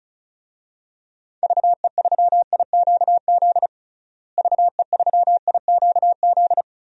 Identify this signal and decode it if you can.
Morse code, heard as a keyed tone, about 36 wpm, 700 Hz. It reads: VE3IQZ VE3IQZ
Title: DXCC-100 Call Signs - 6 Characters - Repeated 3x - 35wpm